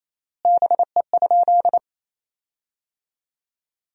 Morse code, heard as a keyed tone, about 28 wpm, 700 Hz. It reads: BE?
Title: 3x Word-Spacing - Sentences from Top 100 Words - 28wpm